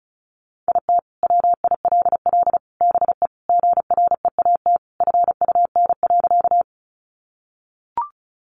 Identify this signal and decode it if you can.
Morse code, heard as a keyed tone, about 34 wpm, 700 Hz. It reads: IT WILL BE GREAT FUN.
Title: Sentences from Top 700 Words 35wpm